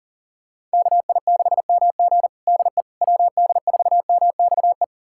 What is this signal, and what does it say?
Morse code, heard as a keyed tone, about 40 wpm, 700 Hz. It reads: KI6MG DE WD4MXE
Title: USA Callsign DE Callsign 40wpm